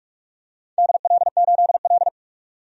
Morse code, heard as a keyed tone, about 44 wpm, 700 Hz. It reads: DL8L
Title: DXCC-100 Call Signs - 4 Characters - 45wpm